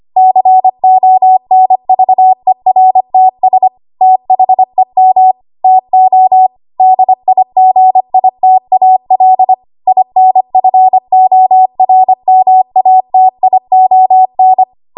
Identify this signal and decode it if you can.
Morse code, heard as a keyed tone, about 25 wpm, 750 Hz. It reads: CONVERTS THEM TO DIGITAL INFORMATION
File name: Archive / 25 WPM / 25WPM